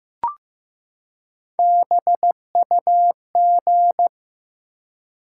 Morse code, heard as a keyed tone, about 15 wpm, 700 Hz. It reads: BUG
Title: Three Letter Words - 15wpm